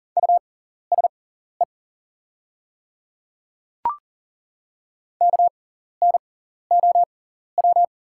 Morse code, heard as a keyed tone, about 38 wpm, spaced as for 14 wpm, 690 Hz. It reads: USE KNOW
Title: 6x Character Spacing - Top 100 Words - 40wpm